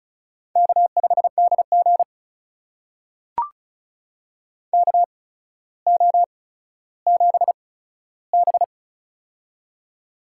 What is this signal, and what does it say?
Morse code, heard as a keyed tone, about 34 wpm, spaced as for 15 wpm, 700 Hz. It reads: K5DG KO7B
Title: USA Call Signs - 4 Characters - 8x Character Spacing - 35wpm